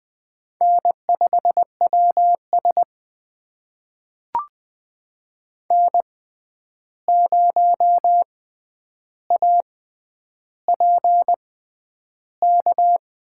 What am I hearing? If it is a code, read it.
Morse code, heard as a keyed tone, about 20 wpm, spaced as for 10 wpm, 700 Hz. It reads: N5WS N0APK
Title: USA Call Signs - 6x Character Spacing - 20wpm